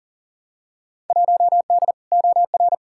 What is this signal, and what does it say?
Morse code, heard as a keyed tone, about 40 wpm, 700 Hz. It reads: 1D OR
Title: Field Day Exchanges - 40wpm